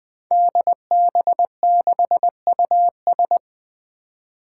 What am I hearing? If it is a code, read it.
Morse code, heard as a keyed tone, about 20 wpm, 700 Hz. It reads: DB6US
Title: Five Challenging Letter-Number Combinations 20wpm